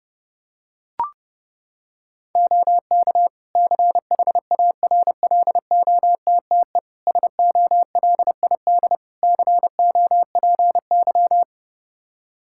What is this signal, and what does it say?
Morse code, heard as a keyed tone, about 30 wpm, 700 Hz. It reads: OK CHARLOTTE SOLID COPY
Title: Practice QSO - Exchange 3 - More Info - 30wpm